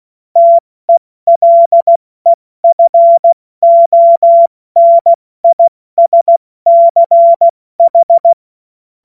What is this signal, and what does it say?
Morse code, heard as a keyed tone, about 16 wpm, 680 Hz. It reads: TELEFONISCH
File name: Morsekurs / Witze / Witz 13. Juni 2023 70 F80